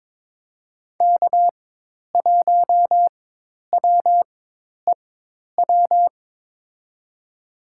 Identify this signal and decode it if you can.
Morse code, heard as a keyed tone, about 22 wpm, spaced as for 10 wpm, 700 Hz. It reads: K1WEW
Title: USA Call Signs - 4x Character Spacing - 22wpm